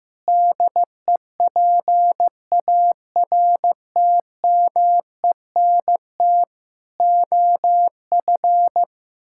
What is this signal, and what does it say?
Morse code, heard as a keyed tone, about 15 wpm, 700 Hz. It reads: DEPARTMENT OF
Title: Sentences from Top 1000 Words - Review Part 5 - 15wpm